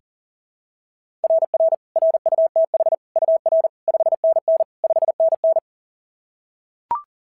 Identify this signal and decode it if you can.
Morse code, heard as a keyed tone, about 39 wpm, 660 Hz. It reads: RR RUTH UR 5NN 5NN
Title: SOTA QSO Practice 40wpm